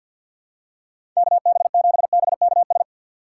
Morse code, heard as a keyed tone, about 46 wpm, 700 Hz. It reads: KB7BKS
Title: USA Call Signs - 3x Character Spacing - 50wpm